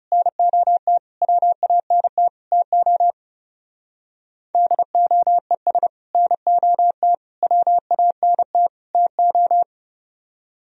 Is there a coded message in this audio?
Yes: Morse code from 0.1 to 9.6 s, heard as a keyed tone, about 38 wpm, 700 Hz. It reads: NOT WANT TO DOES NOT WANT TO